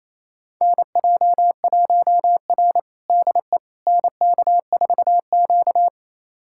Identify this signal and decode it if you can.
Morse code, heard as a keyed tone, about 28 wpm, 700 Hz. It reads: NJ1R DE NK4Q
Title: USA Callsign DE Callsign - 4 Characters - 28wpm